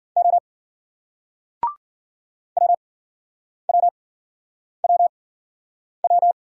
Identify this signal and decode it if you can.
Morse code, heard as a keyed tone, about 54 wpm, spaced as for 9 wpm, 700 Hz. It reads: O WWWW